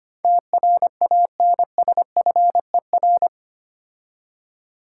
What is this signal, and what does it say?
Morse code, heard as a keyed tone, about 25 wpm, 700 Hz. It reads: TRANSFER
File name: 8 Letter Words - First 3 Letters Given - 25wpm